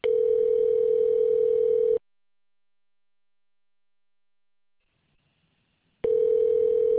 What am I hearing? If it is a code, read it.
Ringback tone (North America), 440 + 480 Hz.